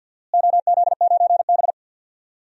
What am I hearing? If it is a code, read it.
Morse code, heard as a keyed tone, about 52 wpm, 700 Hz. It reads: OZ9B
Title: DXCC-100 Call Signs - 4 Characters - Repeated 2x - 50wpm